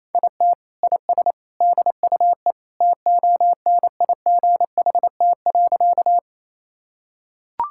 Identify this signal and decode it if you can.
Morse code, heard as a keyed tone, about 28 wpm, 700 Hz. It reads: IT IS DUE TONIGHT.
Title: Sentences from Top 1000 Words - Review Part 6 - 28wpm